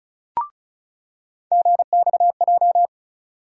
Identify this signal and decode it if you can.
Morse code, heard as a keyed tone, about 34 wpm, 700 Hz. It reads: GXJ
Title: USA Call Sign Postfixes - 3 Letters 35wpm